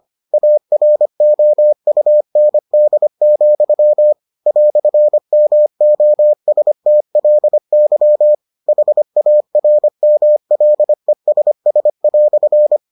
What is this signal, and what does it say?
Morse code, heard as a keyed tone, about 25 wpm, 600 Hz. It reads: AROUND, "MOSTLY HARMLESS"